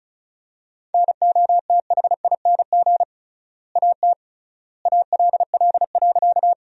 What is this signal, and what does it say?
Morse code, heard as a keyed tone, about 34 wpm, 700 Hz. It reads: NOTHING AT ALL.
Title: 3x Word-Spacing - Sentences from Top 500 Words - 35wpm